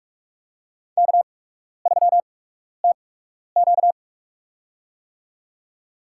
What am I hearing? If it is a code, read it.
Morse code, heard as a keyed tone, about 44 wpm, spaced as for 12 wpm, 700 Hz. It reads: K3TQ
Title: USA Call Signs - 4 Characters - 8x Character Spacing - 45wpm